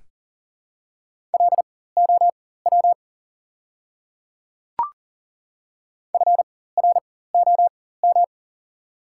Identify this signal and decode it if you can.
Morse code, heard as a keyed tone, about 40 wpm, spaced as for 19 wpm, 700 Hz. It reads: LOW FROM
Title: 4x Character Spacing - Top 300 Words - 40wpm